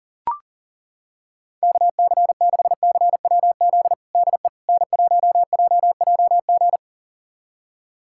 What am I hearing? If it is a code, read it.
Morse code, heard as a keyed tone, about 40 wpm, 700 Hz. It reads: KC6CWZ DE N1JJG
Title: USA Callsign DE Callsign 40wpm